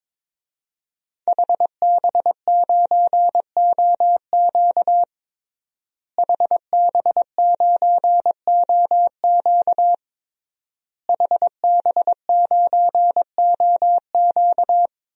Morse code, heard as a keyed tone, about 22 wpm, 700 Hz. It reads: HB9OQ HB9OQ HB9OQ
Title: DXCC-100 Call Signs - 5 Characters - Repeated 3x - 22wpm